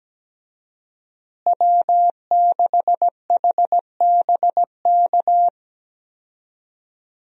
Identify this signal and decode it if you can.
Morse code, heard as a keyed tone, about 17 wpm, 700 Hz. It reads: W6HBK
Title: USA Call Signs - 5 Characters - Repeated 3x - 17wpm